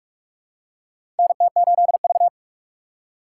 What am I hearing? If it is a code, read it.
Morse code, heard as a keyed tone, about 46 wpm, 700 Hz. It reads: NT8V
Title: USA Call Signs - 4 Characters - 2x Character Spacing - 45wpm